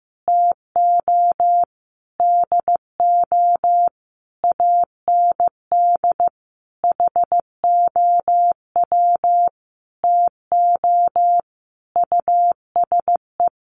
Morse code, heard as a keyed tone, about 15 wpm, 700 Hz. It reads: TO DO AND HOW TO USE
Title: Sentences from Top 500 Words 15wpm